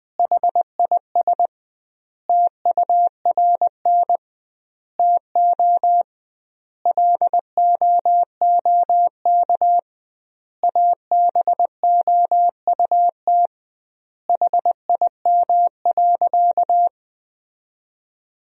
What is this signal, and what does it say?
Morse code, heard as a keyed tone, about 20 wpm, 700 Hz. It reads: HIS TURN TO LOOK ABOUT HIM.
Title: 2x Word-Spacing - Sentences from Top 200 Words - 20wpm